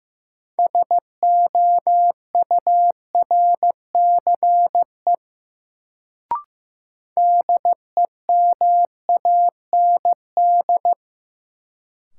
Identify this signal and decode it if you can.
Morse code, heard as a keyed tone, about 15 wpm, 700 Hz. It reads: SOURCE DEMAND
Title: Sentences from Top 800 Words - New Word Drill - 15wpm